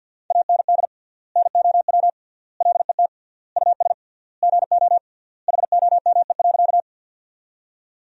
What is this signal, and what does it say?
Morse code, heard as a keyed tone, about 53 wpm, 700 Hz. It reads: AND NOW LET US GO HOME.
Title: 3x Word-Spacing - Sentences from Top 300 Words - 50wpm